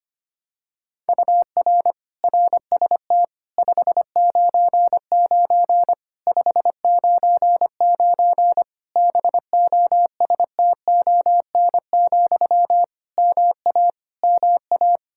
Morse code, heard as a keyed tone, about 25 wpm, 700 Hz. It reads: UR RST 599 599 BOSTON, MA MA
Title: Practice QSO - Exchange 1 - Beginning - 25wpm